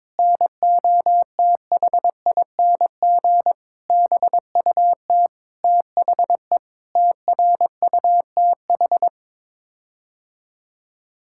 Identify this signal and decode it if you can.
Morse code, heard as a keyed tone, about 22 wpm, 700 Hz. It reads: NOTHING BUT THE TRUTH
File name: practice / 22wpm / Sets of 8 Words - 22wpm